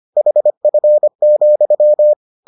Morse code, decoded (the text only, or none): HF,